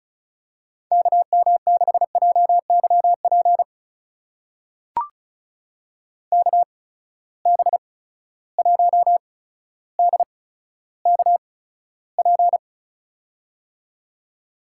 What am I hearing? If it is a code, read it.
Morse code, heard as a keyed tone, about 36 wpm, spaced as for 15 wpm, 700 Hz. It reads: KM6JYP KB1DKP
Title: USA Call Signs - 6 Characters - 8x Character Spacing - 35wpm